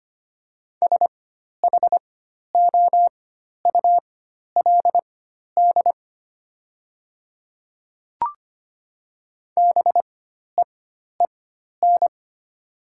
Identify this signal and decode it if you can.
Morse code, heard as a keyed tone, about 25 wpm, spaced as for 12 wpm, 700 Hz. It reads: SHOULD BEEN
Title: 4x Character Spacing - Top 300 Words - 25wpm